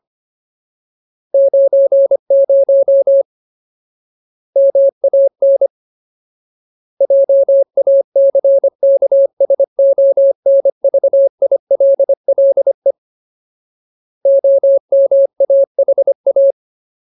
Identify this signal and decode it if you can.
Morse code, heard as a keyed tone, about 25 wpm, 550 Hz. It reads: 90 MAN JACKSONVILLE OMAHA
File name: Ultimate Word List / 25WPM Ultimate